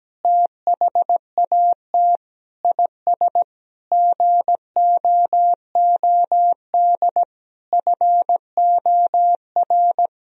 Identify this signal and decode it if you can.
Morse code, heard as a keyed tone, about 17 wpm, 700 Hz. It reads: THAT IS GOOD FOR
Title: Sentences from Top 1000 Words - Review Part 3 - 17wpm